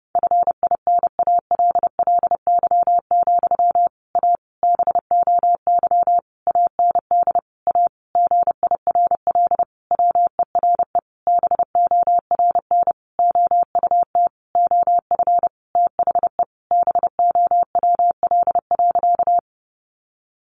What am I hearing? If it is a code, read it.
Morse code, heard as a keyed tone, about 30 wpm, 700 Hz. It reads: FINALLY, A BOY AND A GIRL WERE BORN OUT OF THE BOWL.
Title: Sentences from Top 1200 Words 30wpm